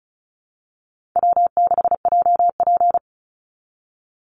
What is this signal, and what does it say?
Morse code, heard as a keyed tone, about 34 wpm, 700 Hz. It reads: W6JP
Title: NA SOTA Chaser Callsigns 35wpm